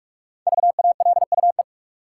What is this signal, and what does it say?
Morse code, heard as a keyed tone, about 46 wpm, 700 Hz. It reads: VALUE
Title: Sentences from Top 600 Words - New Word Drill - 45wpm